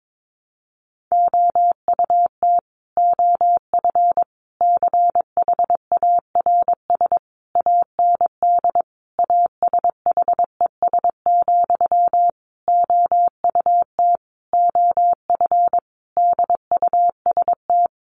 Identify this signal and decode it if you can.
Morse code, heard as a keyed tone, about 22 wpm, 700 Hz. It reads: OUT OF CHARS AND ASHES, OUT OF DUST
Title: A Sound Of Thunder 22wpm